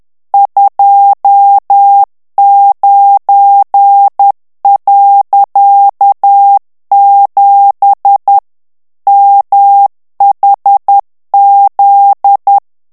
Morse code, heard as a keyed tone, about 10 wpm, 800 Hz. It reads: 29.7 MHZ
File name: ANGLAIS texte clair 3 vitesse 10 mots minute